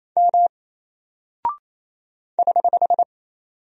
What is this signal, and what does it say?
Morse code, heard as a keyed tone, about 28 wpm, 700 Hz. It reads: M <HH>
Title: Single Character (Letter:Number:Punctuation:Prosign) Alt 28wpm